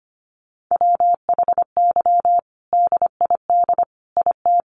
Morse code, heard as a keyed tone, about 25 wpm, 700 Hz. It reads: WHY DID IT
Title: Sentences from Top 400 Words 25wpm